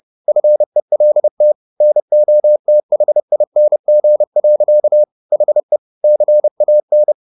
Morse code, heard as a keyed tone, about 30 wpm, 600 Hz. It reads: FELT NOTHING. HE CAN
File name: the old man and the sea / the old man and the sea 1 30wpm